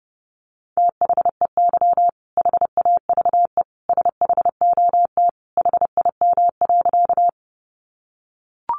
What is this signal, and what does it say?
Morse code, heard as a keyed tone, about 30 wpm, 700 Hz. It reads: THEY HAVE SHOT HIM.
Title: Sentences from Top 1100 Words 30wpm